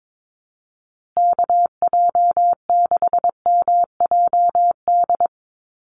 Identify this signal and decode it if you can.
Morse code, heard as a keyed tone, about 22 wpm, 700 Hz. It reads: KJ6MJD
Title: USA Call Signs Repeated 3x 22wpm